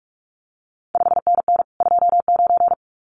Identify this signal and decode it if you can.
Morse code, heard as a keyed tone, about 46 wpm, 700 Hz. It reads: 5NN 29
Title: CQ WW DX Contest Exchange Encore - 45wpm